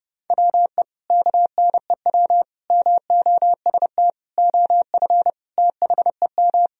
Morse code, heard as a keyed tone, about 32 wpm, 700 Hz. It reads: WE KNEW MOST OF THEM
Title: Sentences from Top 1000 Words - Review Part 1 - 30wpm